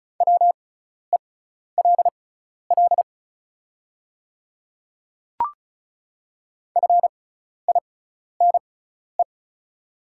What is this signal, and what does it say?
Morse code, heard as a keyed tone, about 34 wpm, spaced as for 12 wpm, 690 Hz. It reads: WELL FINE
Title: 6x Character Spacing - Top 500 Words - 35wpm